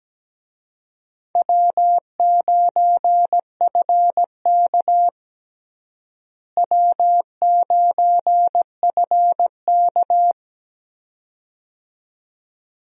Morse code, heard as a keyed tone, about 17 wpm, 700 Hz. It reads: W9FK W9FK
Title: USA Call Signs - 4 Characters - Repeated 2x - 17wpm